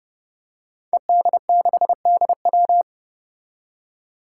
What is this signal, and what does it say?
Morse code, heard as a keyed tone, about 30 wpm, 700 Hz. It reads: ED6DW
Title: Five Challenging Letter-Number Combinations 30wpm